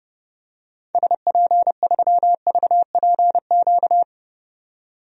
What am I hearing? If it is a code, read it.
Morse code, heard as a keyed tone, about 30 wpm, 700 Hz. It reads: SP3VPQ
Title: DXCC-100 Call Signs - 6 Characters - 30wpm